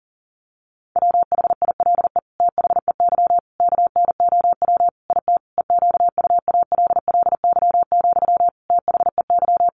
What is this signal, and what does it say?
Morse code, heard as a keyed tone, about 40 wpm, 700 Hz. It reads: WHILE THEY KNOW IT EQUALLY, THEY
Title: Sentences from Top 1400 Words 40wpm